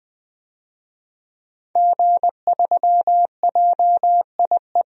Morse code, heard as a keyed tone, about 21 wpm, 700 Hz. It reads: G3JIE
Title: DXCC-100 Call Signs - 5 Characters - Repeated 3x - 20wpm